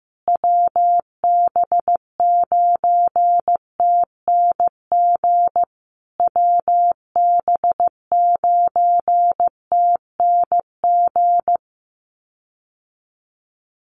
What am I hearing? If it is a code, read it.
Morse code, heard as a keyed tone, about 15 wpm, 700 Hz. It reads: WB9TNG WB9TNG
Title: USA Call Signs Repeated 3x 15wpm